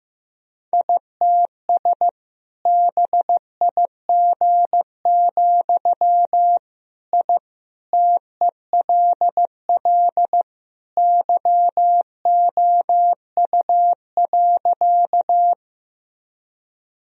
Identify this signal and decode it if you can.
Morse code, heard as a keyed tone, about 15 wpm, 700 Hz. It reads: ITS BIG, I TELL YOU.
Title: Sentences from Top 1000 Words - Review Part 1 - 15wpm